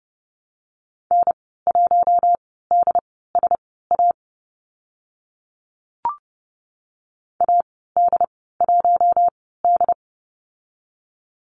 Morse code, heard as a keyed tone, about 30 wpm, spaced as for 17 wpm, 700 Hz. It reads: N1DSA AD1D
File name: Lesson 15g - Number 5 - Call Signs 30wpm